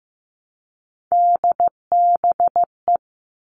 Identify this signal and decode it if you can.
Morse code, heard as a keyed tone, about 15 wpm, 700 Hz. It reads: DBE
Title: All Cut Numbers - 3-Digit 15wpm